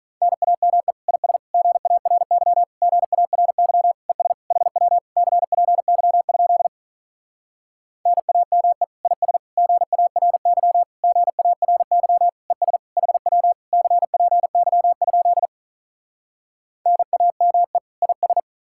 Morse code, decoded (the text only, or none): NAME IS GARY GARY ES HW CPY? NAME IS GARY GARY ES HW CPY? NAME IS